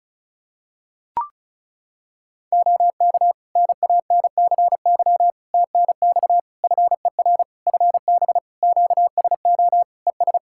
Morse code, heard as a keyed tone, about 36 wpm, 700 Hz. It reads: OK NANCY TNX FER FB QSO ES
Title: Practice QSO - Exchange 4 - Ending - 35wpm